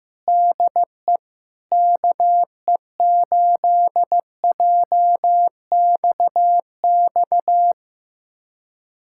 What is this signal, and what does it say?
Morse code, heard as a keyed tone, about 15 wpm, 700 Hz. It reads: DE KE8JXX
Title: USA Callsign DE Callsign - 6 Characters - 15wpm